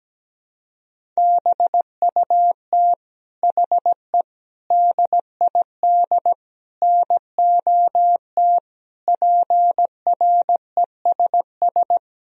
Morse code, heard as a keyed tone, about 17 wpm, 700 Hz. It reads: BUT HE DID NOT PRESS
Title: Sentences from Top 1000 Words - Review Part 1 - 17wpm